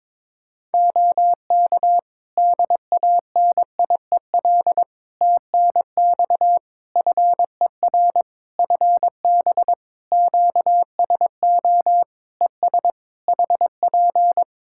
Morse code, heard as a keyed tone, about 22 wpm, 700 Hz. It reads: OK DANIEL TNX FER FB QSO ES HP
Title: Practice QSO - Exchange 4 - Ending - 22wpm